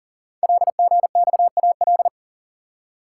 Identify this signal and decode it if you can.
Morse code, heard as a keyed tone, about 44 wpm, 700 Hz. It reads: LGXAL
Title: Code Groups - 5 Letters - 40wpm